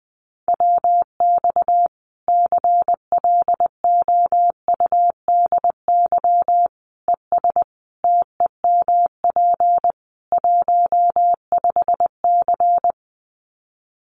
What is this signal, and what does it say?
Morse code, heard as a keyed tone, about 20 wpm, 700 Hz. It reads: WX CLOUDY ES TEMP 15C
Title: Common QSO Phrases 20wpm